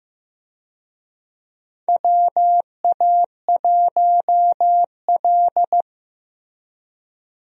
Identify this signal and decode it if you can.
Morse code, heard as a keyed tone, about 15 wpm, 700 Hz. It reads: WA1L
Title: USA Call Signs - 4 Characters - Repeated 3x - 15wpm